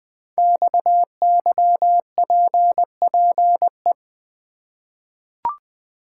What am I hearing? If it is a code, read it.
Morse code, heard as a keyed tone, about 20 wpm, 700 Hz. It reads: XYPPE
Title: Code Groups - 5 Letters - 20wpm